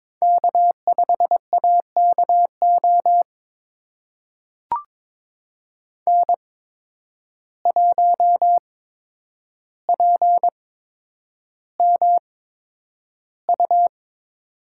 Morse code, heard as a keyed tone, about 22 wpm, spaced as for 9 wpm, 700 Hz. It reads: K5AKO N1PMU
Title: USA Call Signs - 5 Characters - 8x Character Spacing - 22wpm